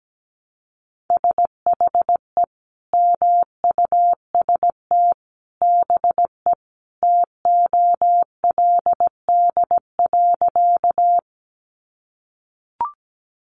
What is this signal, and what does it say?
Morse code, heard as a keyed tone, about 17 wpm, 700 Hz. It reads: SHE MUST BE TOLD.
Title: Sentences from Top 400 Words 17wpm